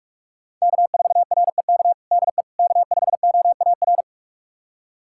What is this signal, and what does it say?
Morse code, heard as a keyed tone, about 46 wpm, 700 Hz. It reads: K4REK DE K5OAR
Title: USA Callsign DE Callsign - 5 Characters - 45wpm